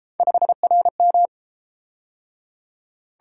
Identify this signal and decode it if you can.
Morse code, heard as a keyed tone, about 29 wpm, 700 Hz. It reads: 5RM